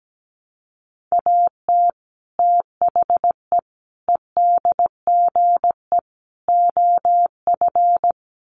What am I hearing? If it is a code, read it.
Morse code, heard as a keyed tone, about 17 wpm, 700 Hz. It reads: AT THE EDGE OF
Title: Sets of 4 Words 17wpm